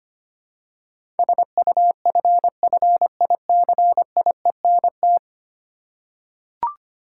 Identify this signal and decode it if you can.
Morse code, heard as a keyed tone, about 25 wpm, 700 Hz. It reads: SUFFICIENT
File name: Ten Letter Words - 25wpm